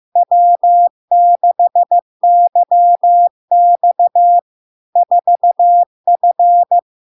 Morse code, heard as a keyed tone, about 15 wpm, 700 Hz. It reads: W6YX 4F
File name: LARK-FD-2022-Log-in-CW-15wpm-700-Hz